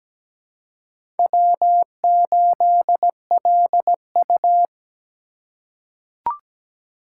Morse code, heard as a keyed tone, about 17 wpm, 700 Hz. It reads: W8LU
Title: USA Call Signs - 4 Characters - 4x Character Spacing - 17wpm